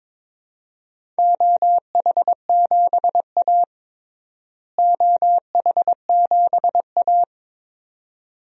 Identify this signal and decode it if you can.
Morse code, heard as a keyed tone, about 22 wpm, 700 Hz. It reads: OH7A OH7A
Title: DXCC-100 Call Signs - 4 Characters - Repeated 2x - 22wpm